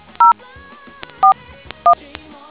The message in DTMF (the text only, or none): 041